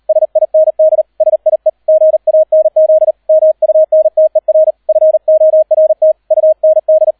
Morse code, heard as a keyed tone, about 38 wpm, 630 Hz, with radio noise. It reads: SIND SIE GANZ MUNTER FORT UND